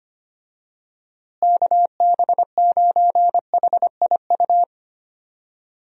Morse code, heard as a keyed tone, about 25 wpm, 700 Hz. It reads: KB9HIU
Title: USA Call Signs - 6 Characters - Repeated 3x - 25wpm